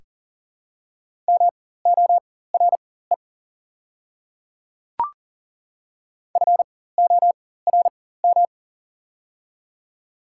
Morse code, heard as a keyed tone, about 40 wpm, spaced as for 19 wpm, 700 Hz. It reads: MORE FORM
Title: 4x Character Spacing - Top 200 Words - 40wpm